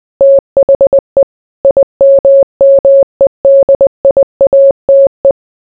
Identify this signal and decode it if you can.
Morse code, heard as a keyed tone, about 20 wpm, 550 Hz. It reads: THE IMMEDIATE